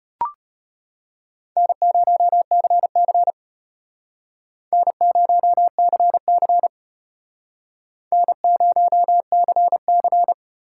Morse code, heard as a keyed tone, about 33 wpm, 700 Hz. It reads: N0CC N0CC N0CC